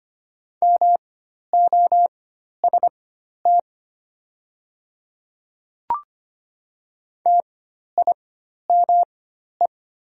Morse code, heard as a keyed tone, about 25 wpm, spaced as for 12 wpm, 700 Hz. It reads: MOST TIME
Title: 4x Character Spacing - Top 300 Words - 25wpm